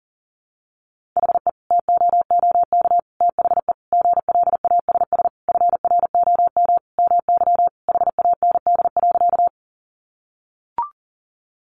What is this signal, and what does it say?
Morse code, heard as a keyed tone, about 40 wpm, 700 Hz. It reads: HE TOOK THE GLASS FROM MY HAND.
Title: Sentences from Top 1300 Words 40wpm